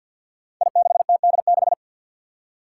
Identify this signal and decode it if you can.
Morse code, heard as a keyed tone, about 53 wpm, 700 Hz. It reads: I6TD6
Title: Five Challenging Letter-Number Combinations 50wpm